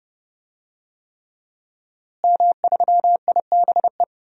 Morse code, heard as a keyed tone, about 30 wpm, 700 Hz. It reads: M3IBE
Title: DXCC-100 Call Signs - 5 Characters - Repeated 3x - 30wpm